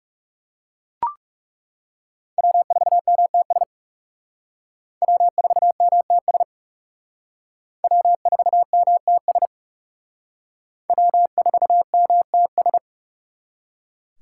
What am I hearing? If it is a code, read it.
Morse code, heard as a keyed tone, about 36 wpm, 700 Hz. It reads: W4MTS W4MTS W4MTS W4MTS
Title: Speed Racing - USA 1x3 Call Signs 30wpm